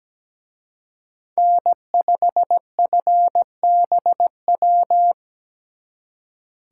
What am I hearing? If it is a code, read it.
Morse code, heard as a keyed tone, about 20 wpm, 700 Hz. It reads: N5FBW
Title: Speed Racing - USA 1x3 Call Signs 17wpm